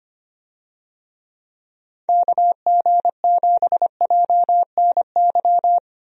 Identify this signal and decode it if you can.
Morse code, heard as a keyed tone, about 25 wpm, 700 Hz. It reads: KG7JNY
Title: USA Call Signs - 6 Characters - Repeated 2x - 25wpm